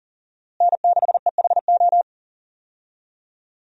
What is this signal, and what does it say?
Morse code, heard as a keyed tone, about 40 wpm, 700 Hz. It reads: N6EHO
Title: USA Call Signs - 5 Characters - Repeated 2x - 40wpm